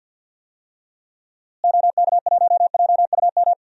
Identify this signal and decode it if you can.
Morse code, heard as a keyed tone, about 52 wpm, 700 Hz. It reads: OK1JUM
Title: DXCC-100 Call Signs - 6 Characters - Repeated 2x - 50wpm